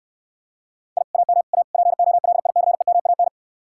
Morse code, heard as a keyed tone, about 69 wpm, 700 Hz. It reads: I AM A POLICEMAN